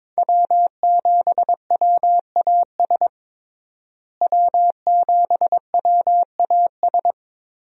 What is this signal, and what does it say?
Morse code, heard as a keyed tone, about 22 wpm, 700 Hz. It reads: W7WAS W7WAS
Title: USA Call Signs - 5 Characters - Repeated 3x - 22wpm